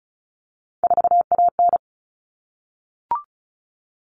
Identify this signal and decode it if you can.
Morse code, heard as a keyed tone, about 34 wpm, 700 Hz. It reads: 4AN
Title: Common Cut Numbers - 3-Digit 35wpm